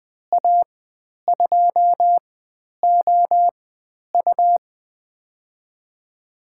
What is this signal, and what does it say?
Morse code, heard as a keyed tone, about 20 wpm, spaced as for 10 wpm, 700 Hz. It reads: A2OU
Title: 102 Lesson 17g - Letter u - Call Signs 20wpm